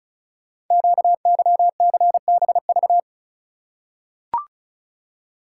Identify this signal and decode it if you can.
Morse code, heard as a keyed tone, about 36 wpm, 690 Hz. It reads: QYCBV
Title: Code Groups - 5 Letters - 35wpm